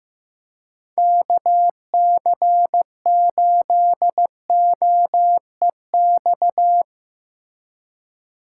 Morse code, heard as a keyed tone, about 15 wpm, 700 Hz. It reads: KC8OEX
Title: USA Call Signs - 6 Characters - 2x Character Spacing - 15wpm